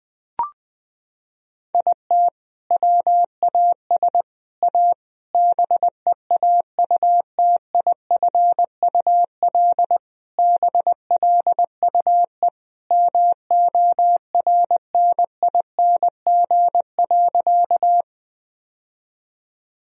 Morse code, heard as a keyed tone, about 20 wpm, 700 Hz. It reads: IT WAS A BEAUTIFUL BLUE MORNING.
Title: Sentences from Top 1000 Words - Review Part 3 - 20wpm